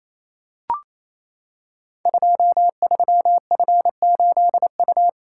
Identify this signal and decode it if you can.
Morse code, heard as a keyed tone, about 29 wpm, 700 Hz. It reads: 23F8U
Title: Code Groups - 5 Letter:Numbers - 28wpm